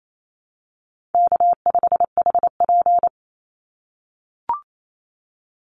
Morse code, heard as a keyed tone, about 28 wpm, 700 Hz. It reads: K5HP
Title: Easy Call Signs - 28wpm